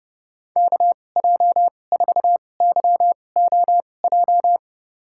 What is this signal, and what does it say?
Morse code, heard as a keyed tone, about 30 wpm, spaced as for 22 wpm, 700 Hz. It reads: KJ4YOJ
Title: USA Call Signs - 6 Characters - 2x Character Spacing - 30wpm